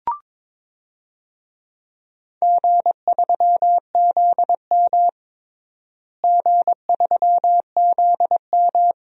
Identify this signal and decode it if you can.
Morse code, heard as a keyed tone, about 22 wpm, 700 Hz. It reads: G3ZM G3ZM
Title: DXCC-100 Call Signs - 4 Characters - Repeated 3x - 22wpm